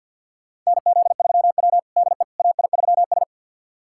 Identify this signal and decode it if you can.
Morse code, heard as a keyed tone, about 53 wpm, 700 Hz. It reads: NZ3W DE AI3S
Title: USA Callsign DE Callsign - 4 Characters - 50wpm